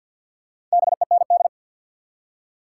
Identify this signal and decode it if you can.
Morse code, heard as a keyed tone, about 53 wpm, 700 Hz. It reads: BEND
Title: Four Letter Words - 50wpm